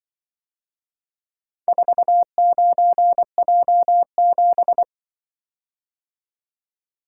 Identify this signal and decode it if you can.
Morse code, heard as a keyed tone, about 24 wpm, 700 Hz. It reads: 49J7
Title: Speed-Racing - Four Letter-Number Combinations - 20wpm